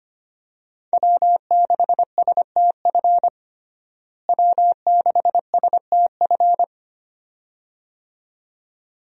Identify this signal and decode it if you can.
Morse code, heard as a keyed tone, about 25 wpm, 700 Hz. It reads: W6STF W6STF
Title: USA Call Signs - 5 Characters - Repeated 2x - 25wpm